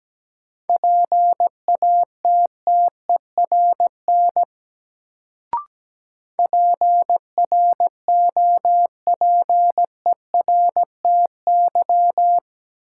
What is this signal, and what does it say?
Morse code, heard as a keyed tone, about 17 wpm, 700 Hz. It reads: PATTERN PROPERTY
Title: Sentences from Top 1000 Words - New Word Drill - 17wpm